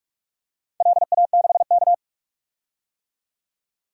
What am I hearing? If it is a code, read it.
Morse code, heard as a keyed tone, about 44 wpm, 700 Hz. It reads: LA6K
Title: DXCC-100 Call Signs - 4 Characters - 45wpm